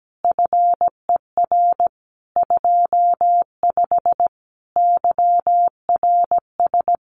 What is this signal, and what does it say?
Morse code, heard as a keyed tone, about 17 wpm, 700 Hz. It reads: FER 25 YRS
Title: Common QSO Phrases Encore 17wpm